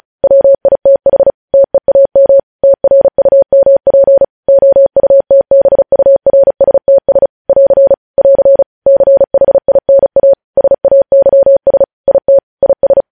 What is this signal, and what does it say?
Morse code, heard as a keyed tone, about 36 wpm, 550 Hz. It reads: WITH TEAM TRUMP OUTBURSTS <AR> <AR> CHINA SAYS IT IS
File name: podcast / News headlines-1x-35x35